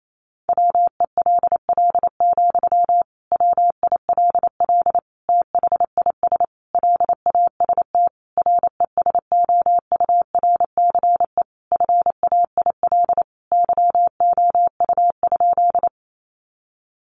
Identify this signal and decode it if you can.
Morse code, heard as a keyed tone, about 28 wpm, 700 Hz. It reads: WELL, WILL THIS LAST RESOURCE FAIL YOU?
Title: Sentences from Top 1300 Words 28wpm